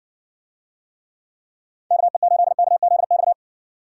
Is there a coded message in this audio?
Yes: Morse code from 1.9 to 3.3 s, heard as a keyed tone, about 60 wpm, 700 Hz. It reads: KE8KZX